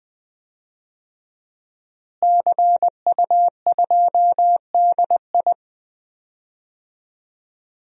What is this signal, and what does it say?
Morse code, heard as a keyed tone, about 20 wpm, 700 Hz. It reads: CU2DI
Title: DXCC-100 Call Signs - 5 Characters - Repeated 2x - 20wpm